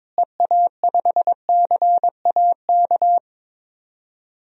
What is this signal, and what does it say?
Morse code, heard as a keyed tone, about 22 wpm, 700 Hz. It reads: EA5CAK
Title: DXCC-100 Call Signs - 6 Characters - Repeated 3x - 22wpm